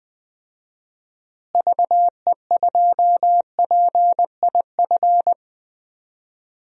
Morse code, heard as a keyed tone, about 20 wpm, 700 Hz. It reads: VE2PIF
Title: DXCC-100 Call Signs - 6 Characters - Repeated 2x - 20wpm